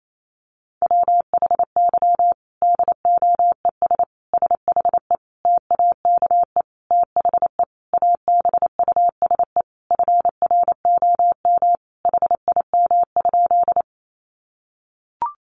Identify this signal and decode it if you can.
Morse code, heard as a keyed tone, about 28 wpm, 700 Hz. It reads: WHY DOES SHE TAKE THE ABUSE FROM HIM?
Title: Sentences from Top 1300 Words 28wpm